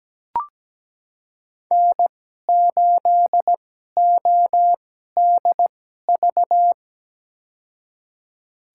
Morse code, heard as a keyed tone, about 17 wpm, spaced as for 12 wpm, 700 Hz. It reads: N8ODV
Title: USA Call Signs - 5 Characters - 2x Character Spacing - 17wpm